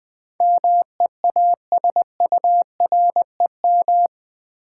Morse code, heard as a keyed tone, about 20 wpm, 700 Hz. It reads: MEASUREM
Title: Build A Word - 4 Letters At A Time - 11-Letter Words - 20wpm